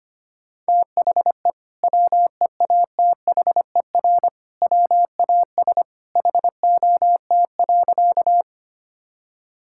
Morse code, heard as a keyed tone, about 25 wpm, 700 Hz. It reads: THE WEATHER WAS HOT.
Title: Sentences from Top 1000 Words - Review Part 5 - 25wpm